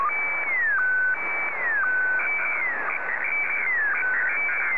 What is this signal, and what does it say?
SSTV (slow-scan television), Scottie DX.